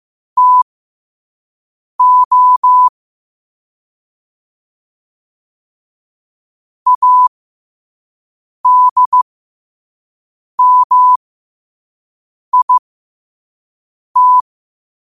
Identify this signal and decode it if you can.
Morse code, heard as a keyed tone, about 15 wpm, spaced as for 5 wpm, 1000 Hz. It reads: TO ADMIT